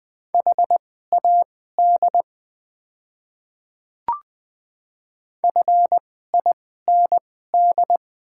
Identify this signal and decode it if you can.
Morse code, heard as a keyed tone, about 20 wpm, spaced as for 14 wpm, 700 Hz. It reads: HAD FIND
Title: 2x Character Spacing - Top 100 Words - 20wpm